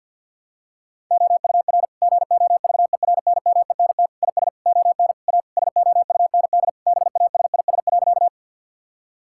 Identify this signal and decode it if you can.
Morse code, heard as a keyed tone, about 50 wpm, 700 Hz. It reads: OUR GOVERNMENT IS ON A SOUND BASIS.
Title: Sentences from Top 1000 Words - Review Part 5 - 50wpm